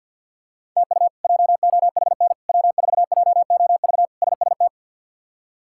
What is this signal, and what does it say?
Morse code, heard as a keyed tone, about 51 wpm, 700 Hz. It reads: TU JOHN W4JOV SST
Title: K1USN SST QSO Practice - 50wpm